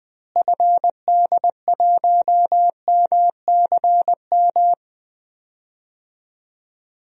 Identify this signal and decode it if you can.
Morse code, heard as a keyed tone, about 20 wpm, 700 Hz. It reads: FD1MCM
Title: DXCC-100 Call Signs - 6 Characters - Repeated 3x - 20wpm